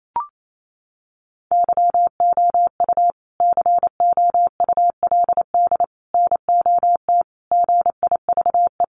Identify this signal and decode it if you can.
Morse code, heard as a keyed tone, about 28 wpm, 700 Hz. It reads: YOU COULD NOT GIVE
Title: Sentences from Top 1500 Words 28wpm